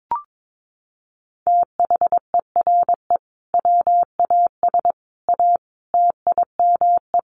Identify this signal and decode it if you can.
Morse code, heard as a keyed tone, about 22 wpm, 700 Hz. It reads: THERE WAS A TIME
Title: Sets of 5 Words 22wpm